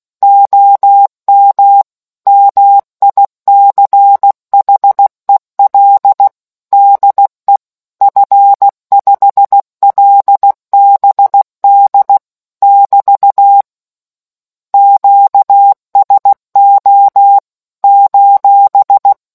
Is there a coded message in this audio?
Yes: Morse code from 0.2 to 19.1 s, heard as a keyed tone, about 16 wpm, 800 Hz. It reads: OM MICHEL DE F5LBD <BT> QSO :